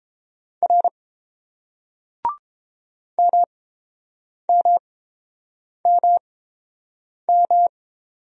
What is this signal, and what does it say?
Morse code, heard as a keyed tone, about 30 wpm, spaced as for 7 wpm, 700 Hz. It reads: R MMMM